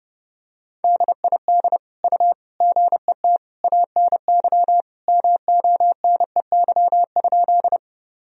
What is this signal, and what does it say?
Morse code, heard as a keyed tone, about 30 wpm, 700 Hz. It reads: DID U GET ANY MONEY?
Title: Sentences from Top 400 Words and Abbreviations 30wpm